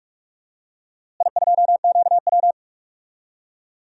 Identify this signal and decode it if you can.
Morse code, heard as a keyed tone, about 42 wpm, 700 Hz. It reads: I2QW
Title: DXCC-100 Call Signs - 4 Characters - Repeated 2x - 45wpm